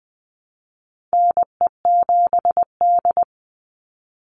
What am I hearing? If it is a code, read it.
Morse code, heard as a keyed tone, about 20 wpm, 700 Hz. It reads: NE7D
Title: Call Signs - 20wpm